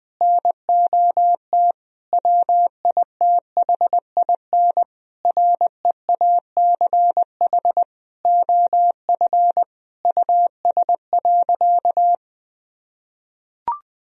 Morse code, heard as a keyed tone, about 20 wpm, 700 Hz. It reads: NOT WITHIN REACH OF US.
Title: Sentences from Top 1000 Words - Review Part 4 - 20wpm